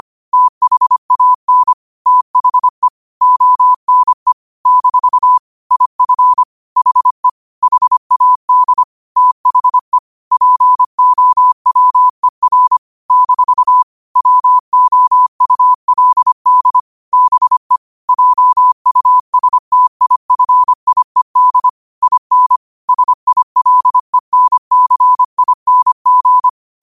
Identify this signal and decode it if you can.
Morse code, heard as a keyed tone, about 25 wpm, 1000 Hz. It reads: THAN THE ONE - IF HE HAD THE POWER - WOULD BE JUSTIFIED IN SILENCING